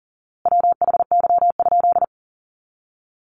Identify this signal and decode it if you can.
Morse code, heard as a keyed tone, about 40 wpm, 700 Hz. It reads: WHY?
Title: A Sound Of Thunder 40wpm